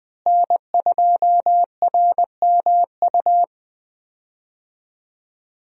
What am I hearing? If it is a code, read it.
Morse code, heard as a keyed tone, about 20 wpm, 700 Hz. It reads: N2RMU
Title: USA Call Signs - 5 Characters - Repeated 2x - 20wpm